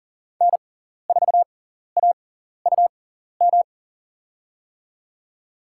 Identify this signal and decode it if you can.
Morse code, heard as a keyed tone, about 39 wpm, spaced as for 14 wpm, 700 Hz. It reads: N4AUM
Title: USA Call Signs - 5 Characters - 6x Character Spacing - 40wpm